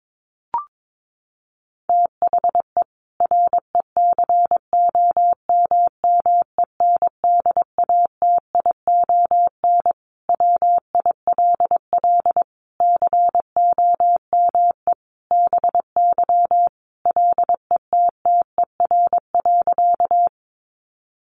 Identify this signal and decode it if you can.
Morse code, heard as a keyed tone, about 22 wpm, 700 Hz. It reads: THE RECOMMENDATION WILL COME BY LETTER.
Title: Sentences from Top 1400 Words 22wpm